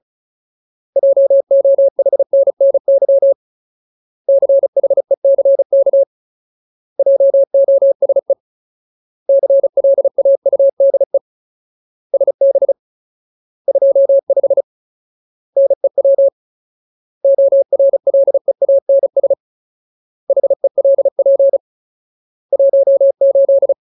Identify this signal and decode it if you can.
Morse code, heard as a keyed tone, about 34 wpm, 550 Hz. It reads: JOHNNY CHECK JOSE CLAUDE SB 25 NEW ORLEANS HELP 18
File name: Ultimate Word List / 35WPM Ultimate